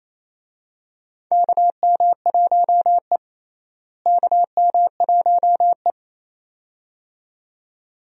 Morse code, heard as a keyed tone, about 29 wpm, 700 Hz. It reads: KM1E KM1E
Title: USA Call Signs - 4 Characters - Repeated 2x - 28wpm